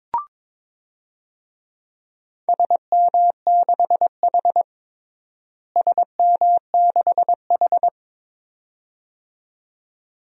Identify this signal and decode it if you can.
Morse code, heard as a keyed tone, about 22 wpm, 700 Hz. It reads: SM6H SM6H
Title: DXCC-100 Call Signs - 4 Characters - Repeated 2x - 22wpm